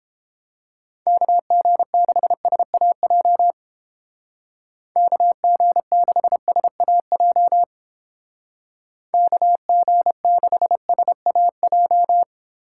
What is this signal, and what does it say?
Morse code, heard as a keyed tone, about 30 wpm, 700 Hz. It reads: KG6SAJ KG6SAJ KG6SAJ